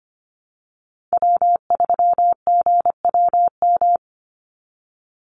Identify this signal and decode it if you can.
Morse code, heard as a keyed tone, about 25 wpm, 700 Hz. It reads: W3GWM
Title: Call Signs Repeated 2x - 25wpm